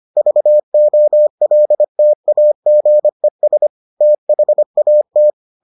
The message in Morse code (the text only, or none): VOLTAGES THAT